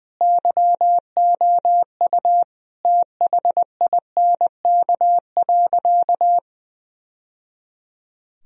Morse code, heard as a keyed tone, about 20 wpm, 700 Hz. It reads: YOU THINK.
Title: Sentences from Top 1000 Words - Review Part 2 - 20wpm